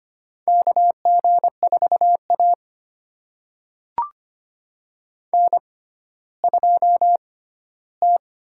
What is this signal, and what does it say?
Morse code, heard as a keyed tone, about 25 wpm, spaced as for 14 wpm, 700 Hz. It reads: KG4A N2T
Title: USA Call Signs - 6x Character Spacing - 25wpm